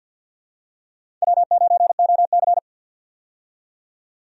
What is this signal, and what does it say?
Morse code, heard as a keyed tone, about 52 wpm, 700 Hz. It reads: W9OC
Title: USA Call Signs - 4 Characters - 50wpm